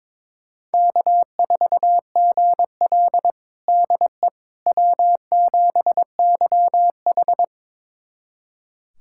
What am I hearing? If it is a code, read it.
Morse code, heard as a keyed tone, about 22 wpm, 700 Hz. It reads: K4GL DE W7YH
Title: USA Callsign DE Callsign - 4 Characters - 22wpm